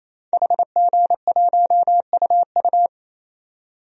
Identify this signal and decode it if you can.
Morse code, heard as a keyed tone, about 28 wpm, 700 Hz. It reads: HG1UU
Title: DXCC-100 Call Signs - 5 Characters - 28wpm